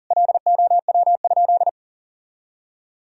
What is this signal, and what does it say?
Morse code, heard as a keyed tone, about 40 wpm, 700 Hz. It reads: LOW?
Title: Sentences from Top 1000 Words - Review Part 1 - 40wpm